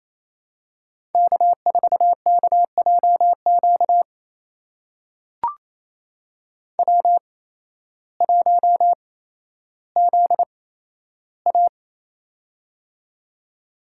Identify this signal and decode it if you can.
Morse code, heard as a keyed tone, about 28 wpm, spaced as for 13 wpm, 700 Hz. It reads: K4KJQ W1ZA
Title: USA Call Signs - 8x Character Spacing - 28wpm